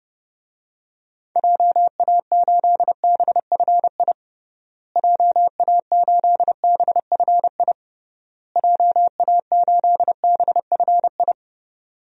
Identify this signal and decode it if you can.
Morse code, heard as a keyed tone, about 30 wpm, 700 Hz. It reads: JA8BFI JA8BFI JA8BFI
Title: DXCC-100 Call Signs - 6 Characters - Repeated 3x - 30wpm